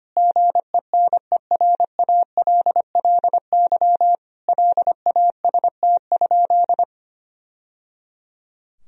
Morse code, heard as a keyed tone, about 25 wpm, 700 Hz. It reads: GENERALLY LAST?
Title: Sentences from Top 1000 Words - Review Part 4 - 25wpm